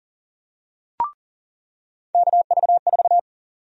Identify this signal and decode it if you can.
Morse code, heard as a keyed tone, about 37 wpm, 700 Hz. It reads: KV4
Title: USA Call Sign Prefixes - 40wpm